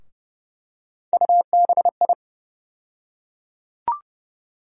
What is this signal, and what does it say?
Morse code, heard as a keyed tone, about 30 wpm, 700 Hz. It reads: UBI
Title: USA Call Sign Postfixes - 3 Letters 30wpm